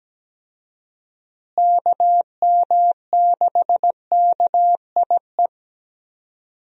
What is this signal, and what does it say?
Morse code, heard as a keyed tone, about 17 wpm, 700 Hz. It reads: KM6KIE
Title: USA Call Signs - 6 Characters - Repeated 3x - 17wpm